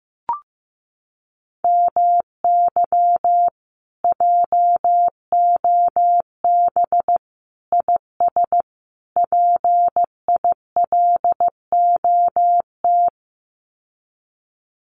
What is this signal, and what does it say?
Morse code, heard as a keyed tone, about 15 wpm, 700 Hz. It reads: MY JOB IS PILOT
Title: Common QSO Phrases Encore 15wpm